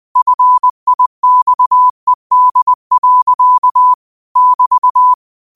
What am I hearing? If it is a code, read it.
Morse code, heard as a keyed tone, about 20 wpm, 1000 Hz. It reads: FIXED. <BT>